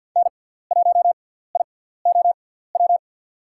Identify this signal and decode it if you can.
Morse code, heard as a keyed tone, about 51 wpm, spaced as for 17 wpm, 700 Hz. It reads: N1IOW
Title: USA Call Signs - 6x Character Spacing - 50wpm